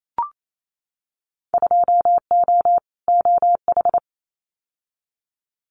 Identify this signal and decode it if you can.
Morse code, heard as a keyed tone, about 28 wpm, 700 Hz. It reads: 2O OH
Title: Winter Field Day Exchanges 28wpm